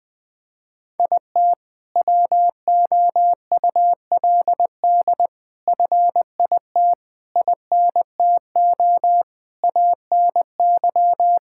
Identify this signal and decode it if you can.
Morse code, heard as a keyed tone, about 20 wpm, 700 Hz. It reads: IT WOULD FIT INTO ANY
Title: Sentences from Top 1000 Words - Review Part 5 - 20wpm